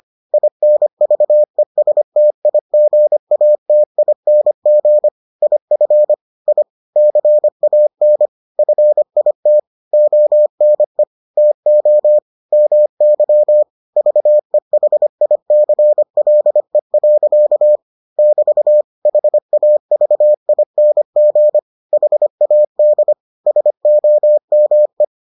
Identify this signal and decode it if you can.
Morse code, heard as a keyed tone, about 25 wpm, 600 Hz. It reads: INVESTIGATING IF I CAN FIT ONE TO MY VEHICLE. <BT> HAVING HAD SOME